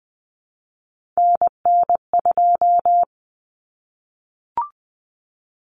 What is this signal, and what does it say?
Morse code, heard as a keyed tone, about 20 wpm, 700 Hz. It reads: NN2
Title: Common Cut Numbers - 3-Digit 20wpm